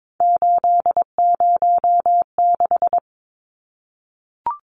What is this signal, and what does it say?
Morse code, heard as a keyed tone, about 22 wpm, 700 Hz. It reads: 806
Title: Numbers - 3-Digit 22wpm